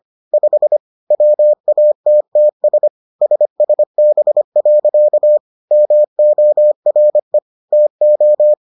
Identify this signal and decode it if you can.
Morse code, heard as a keyed tone, about 25 wpm, 600 Hz. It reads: 5 WATTS SSB. MORE TO